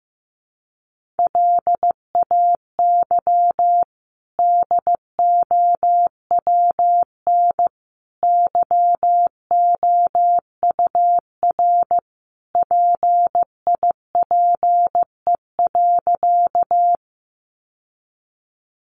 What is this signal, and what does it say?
Morse code, heard as a keyed tone, about 15 wpm, 700 Hz. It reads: LAY DOWN YOUR PIPE.
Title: Sentences from Top 1500 Words 15wpm